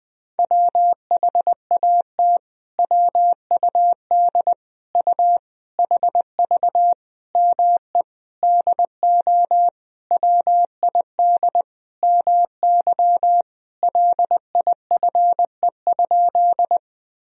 Morse code, heard as a keyed tone, about 21 wpm, 700 Hz. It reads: WHAT WUD U HV ME DO WID MY LIFE?
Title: Sentences from Top 300 Words and Abbreviations 20wpm